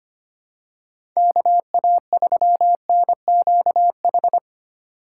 Morse code, heard as a keyed tone, about 25 wpm, 700 Hz. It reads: KA3NQH
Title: USA Call Signs - 2x Character Spacing - 25wpm